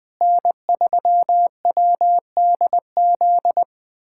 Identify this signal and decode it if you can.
Morse code, heard as a keyed tone, about 20 wpm, 700 Hz. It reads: N3WDZ
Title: USA Callsign DE Callsign 20wpm